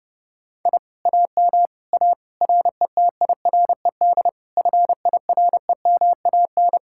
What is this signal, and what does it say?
Morse code, heard as a keyed tone, about 37 wpm, 700 Hz. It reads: I AM A RETIRED FIREMAN
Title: Speed-Racing Common QSO Phrases - 30wpm